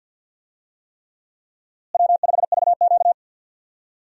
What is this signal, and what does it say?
Morse code, heard as a keyed tone, about 51 wpm, 700 Hz. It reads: W5VQ
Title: USA Call Signs - 4 Characters - Repeated 3x - 50wpm